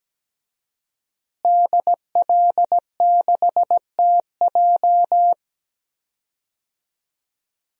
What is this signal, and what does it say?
Morse code, heard as a keyed tone, about 17 wpm, 700 Hz. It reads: DL6TJ
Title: DXCC-100 Call Signs - 5 Characters - Repeated 3x - 17wpm